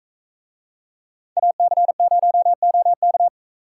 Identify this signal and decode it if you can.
Morse code, heard as a keyed tone, about 41 wpm, 700 Hz. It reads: AC0OK